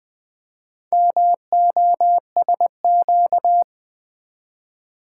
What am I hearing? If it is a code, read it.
Morse code, heard as a keyed tone, about 20 wpm, 700 Hz. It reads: MOSQ
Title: Build A Word - 2 Letters At A Time - 8-Letter Words - 20wpm